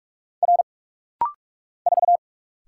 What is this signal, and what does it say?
Morse code, heard as a keyed tone, about 46 wpm, 700 Hz. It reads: R 4
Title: Single Letter-Number - Rapid-Fire 45wpm